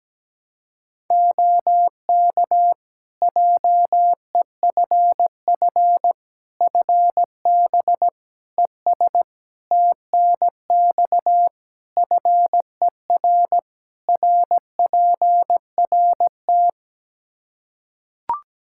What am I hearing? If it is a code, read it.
Morse code, heard as a keyed tone, about 17 wpm, 700 Hz. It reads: OK JEFF FB ES TNX FER RPRT
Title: K1USN SST QSO Practice 17wpm